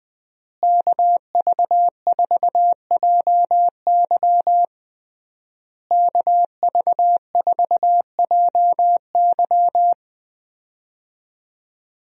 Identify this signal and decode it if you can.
Morse code, heard as a keyed tone, about 20 wpm, 700 Hz. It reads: KV4JY KV4JY
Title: USA Call Signs - 5 Characters - Repeated 3x - 20wpm